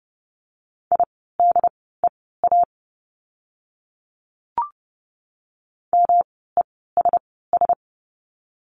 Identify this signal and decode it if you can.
Morse code, heard as a keyed tone, about 30 wpm, spaced as for 17 wpm, 700 Hz. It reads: IDEA MESS
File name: course / 30:10wpm / 108 Lesson 18f - Letter m - Words 30wpm